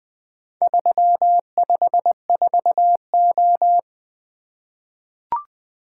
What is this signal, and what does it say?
Morse code, heard as a keyed tone, about 20 wpm, 700 Hz. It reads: 354O
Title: Four Letter-Number Combinations - 20wpm